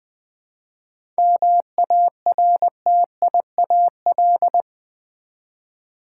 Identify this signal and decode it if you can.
Morse code, heard as a keyed tone, about 20 wpm, 700 Hz. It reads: MARTIAL
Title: 7 Letter Words - First 3 Letters Given - 20wpm